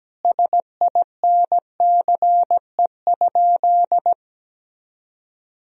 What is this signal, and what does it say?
Morse code, heard as a keyed tone, about 17 wpm, 700 Hz. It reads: SINCE?
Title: Sentences from Top 1000 Words - Review Part 4 - 17wpm